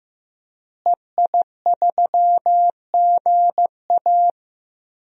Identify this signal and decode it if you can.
Morse code, heard as a keyed tone, about 15 wpm, 700 Hz. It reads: EI3GA
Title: DXCC-100 Call Signs - 5 Characters - Repeated 2x - 15wpm